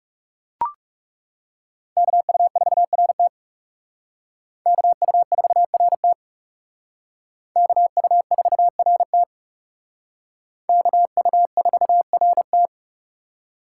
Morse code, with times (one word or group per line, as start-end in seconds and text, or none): 2.0-3.3 KU4RT
4.7-6.1 KU4RT
7.6-9.2 KU4RT
10.7-12.7 KU4RT